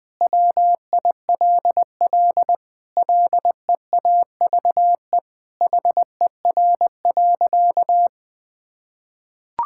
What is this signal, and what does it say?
Morse code, heard as a keyed tone, about 20 wpm, 700 Hz. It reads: WILL LEAVE HER.
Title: Sentences from Top 1000 Words - Review Part 5 - 20wpm